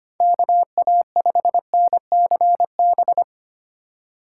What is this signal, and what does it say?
Morse code, heard as a keyed tone, about 25 wpm, 700 Hz. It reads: KA5NCB
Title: USA Call Signs - 2x Character Spacing - 25wpm